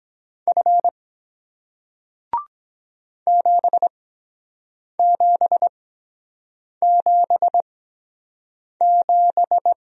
Morse code, heard as a keyed tone, about 23 wpm, spaced as for 7 wpm, 700 Hz. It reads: F 7777